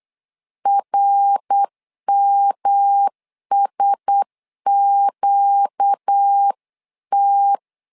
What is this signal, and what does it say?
Morse code, heard as a keyed tone, about 8 wpm, 800 Hz. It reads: RMSQT
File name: LEZIONI / 4 Q1S 8wpm 2710